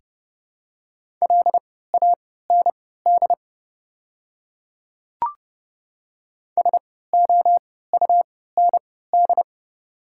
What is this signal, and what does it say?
Morse code, heard as a keyed tone, about 30 wpm, spaced as for 17 wpm, 700 Hz. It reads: LAND SOUND
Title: 3x Character Spacing - Top 200 Words - 30wpm